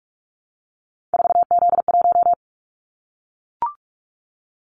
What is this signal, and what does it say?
Morse code, heard as a keyed tone, about 44 wpm, 700 Hz. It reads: 4Z1
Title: Callsign Prefixes for Top 50 DXCC - 45wpm